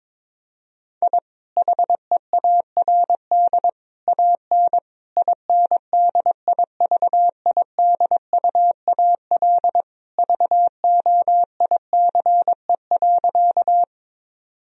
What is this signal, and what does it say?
Morse code, heard as a keyed tone, about 22 wpm, 700 Hz. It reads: I HEARD AN INDIVIDUAL VOICE.
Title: Sentences from Top 1000 Words - Review Part 4 - 22wpm